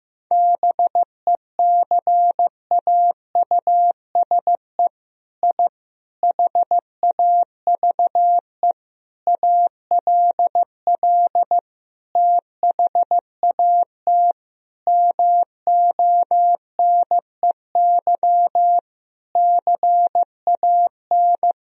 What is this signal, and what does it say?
Morse code, heard as a keyed tone, about 15 wpm, 700 Hz. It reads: BECAUSE I HAVE ALL THAT MONEY CAN
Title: Sentences from Top 1000 Words - Review Part 4 - 15wpm